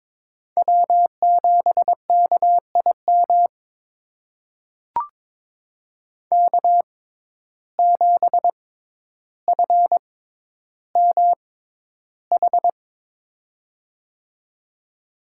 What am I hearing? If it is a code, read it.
Morse code, heard as a keyed tone, about 22 wpm, spaced as for 11 wpm, 700 Hz. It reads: W7KIM K7FMH
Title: USA Call Signs - 5 Characters - 6x Character Spacing - 22wpm